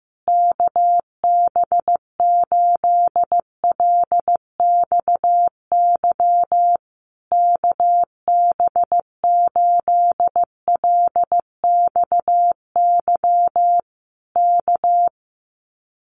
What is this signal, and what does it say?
Morse code, heard as a keyed tone, about 15 wpm, 700 Hz. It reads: KB8LXY KB8LXY K
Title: Common QSO Phrases Encore 15wpm